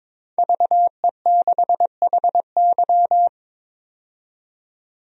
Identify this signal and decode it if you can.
Morse code, heard as a keyed tone, about 22 wpm, 700 Hz. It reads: VE6HY
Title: DXCC-100 Call Signs - 5 Characters - 22wpm